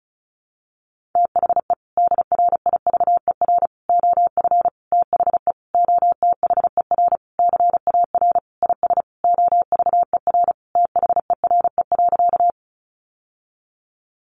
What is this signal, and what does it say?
Morse code, heard as a keyed tone, about 34 wpm, 700 Hz. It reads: THE DRIVER OF THE OTHER CAR IS OVER THERE.
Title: Sentences from Top 1000 Words 35wpm